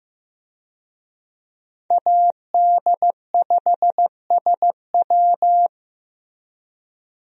Morse code, heard as a keyed tone, about 15 wpm, 700 Hz. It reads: AD5SW
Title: USA Call Signs - 5 Characters - Repeated 3x - 15wpm